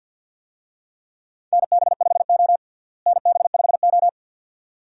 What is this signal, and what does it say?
Morse code, heard as a keyed tone, about 51 wpm, 700 Hz. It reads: NB5O NB5O
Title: USA Call Signs - 4 Characters - Repeated 2x - 50wpm